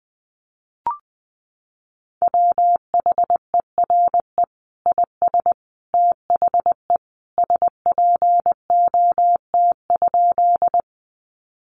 Morse code, heard as a keyed tone, about 20 wpm, 700 Hz. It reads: WHERE IS THE SPOT?
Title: Sentences from Top 1100 Words 20wpm